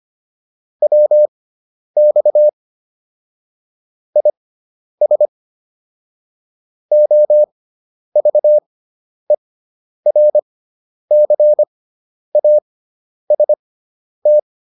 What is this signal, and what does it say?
Morse code, heard as a keyed tone, about 25 wpm, spaced as for 10 wpm, 600 Hz. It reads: WX IS OVERCAST
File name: QSO309 10